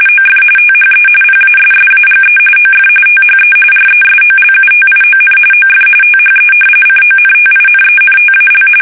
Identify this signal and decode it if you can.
RTTY, about 45.45 baud. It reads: OF THE PASSAGE OF THE RESOLUTION OF THE CONTINENTAL
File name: fsk850 / usnavy1